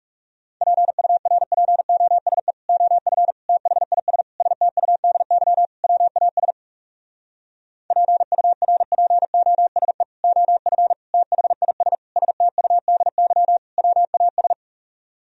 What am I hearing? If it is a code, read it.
Morse code, heard as a keyed tone, about 40 wpm, 700 Hz. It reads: PURPOSE OF THIS STUDY WAS PURPOSE OF THIS STUDY WAS